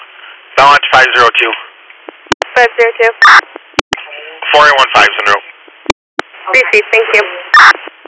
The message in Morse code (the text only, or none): none